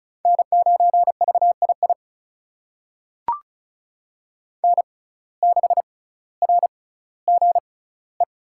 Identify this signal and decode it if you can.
Morse code, heard as a keyed tone, about 36 wpm, spaced as for 18 wpm, 700 Hz. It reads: N9VII N6RGE
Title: USA Call Signs - 5 Characters - 6x Character Spacing - 35wpm